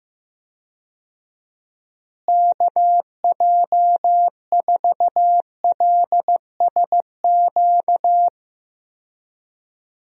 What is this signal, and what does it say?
Morse code, heard as a keyed tone, about 15 wpm, 700 Hz. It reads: KJ4LSQ
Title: USA Call Signs - 6 Characters - Repeated 2x - 15wpm